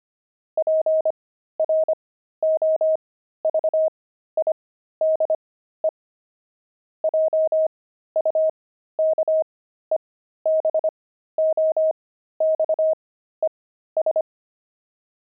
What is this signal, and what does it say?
Morse code, heard as a keyed tone, about 25 wpm, spaced as for 13 wpm, 630 Hz. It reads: PROVIDE JUKEBOXES
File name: pangram-fs-13-cs-25-jukeboxes